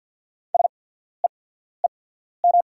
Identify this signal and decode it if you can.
Morse code, heard as a keyed tone, about 48 wpm, spaced as for 14 wpm, 690 Hz. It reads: SEEM
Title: 8x Character Spacing - Top 300 Words - 50wpm